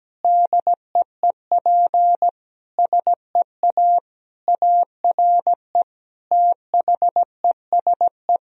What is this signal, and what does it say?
Morse code, heard as a keyed tone, about 17 wpm, 700 Hz. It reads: DEEP SEA ARE THESE
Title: Sentences from Top 1000 Words - Review Part 3 - 17wpm